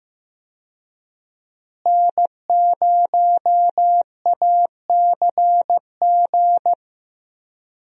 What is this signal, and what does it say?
Morse code, heard as a keyed tone, about 16 wpm, 700 Hz. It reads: N0ACG
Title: USA Call Signs - 5 Characters - Repeated 3x - 15wpm